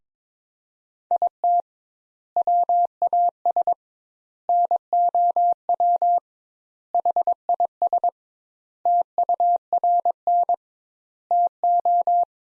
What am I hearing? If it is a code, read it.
Morse code, heard as a keyed tone, about 22 wpm, 700 Hz. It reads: IT WAS NOW HIS TURN TO
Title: Sentences Top 200 Words - 22wpm